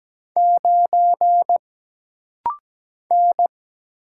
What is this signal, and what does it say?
Morse code, heard as a keyed tone, about 17 wpm, 700 Hz. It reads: 9 N
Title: Single Letter-Number Alt - Rapid-Fire 17wpm